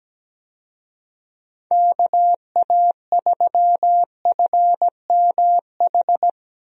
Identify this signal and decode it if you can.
Morse code, heard as a keyed tone, about 17 wpm, 700 Hz. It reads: KA3FMH
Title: USA Call Signs - 6 Characters - Repeated 2x - 17wpm